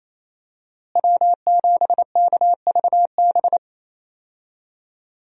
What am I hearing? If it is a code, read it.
Morse code, heard as a keyed tone, about 28 wpm, 700 Hz. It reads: W7KVB
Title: USA Call Signs - 5 Characters - Repeated 3x - 28wpm